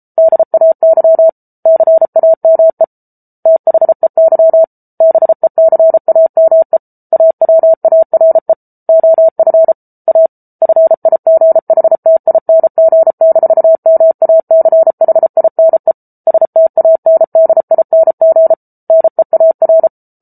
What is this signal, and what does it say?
Morse code, heard as a keyed tone, about 34 wpm, 650 Hz. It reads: DAY CAME THEY BECAME AWARE OF A FIGHTING-MACHINE STANDING NEAR